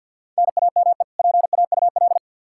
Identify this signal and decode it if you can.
Morse code, heard as a keyed tone, about 46 wpm, 700 Hz. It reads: NAME PAUL
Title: Practice QSO - Exchange 1 - Beginning - 50wpm